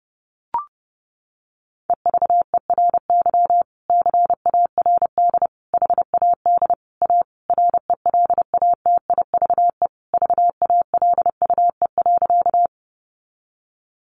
Morse code, heard as a keyed tone, about 30 wpm, 700 Hz. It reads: EVERY CARD HAD A RELATIVE VALUE.
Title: Sentences from Top 1400 Words 30wpm